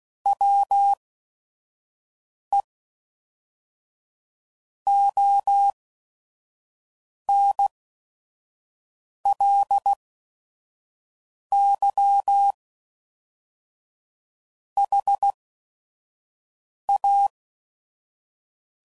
Morse code, heard as a keyed tone, about 16 wpm, spaced as for 5 wpm, 800 Hz. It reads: WE ONLY HA
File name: CW Code Groups @5WPM / Technical Subjects / AND Gate